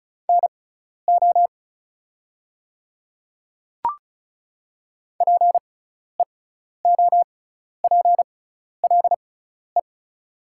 Morse code, heard as a keyed tone, about 36 wpm, spaced as for 12 wpm, 700 Hz. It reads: NO PEOPLE
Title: 6x Character Spacing - Top 200 Words - 35wpm